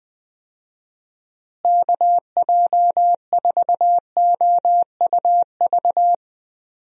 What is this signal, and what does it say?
Morse code, heard as a keyed tone, about 20 wpm, 700 Hz. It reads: KJ4OUV
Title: USA Call Signs - 6 Characters - Repeated 3x - 20wpm